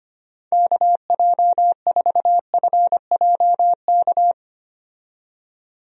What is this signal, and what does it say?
Morse code, heard as a keyed tone, about 25 wpm, 700 Hz. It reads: KJ4FJK
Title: USA Call Signs - 6 Characters - 25wpm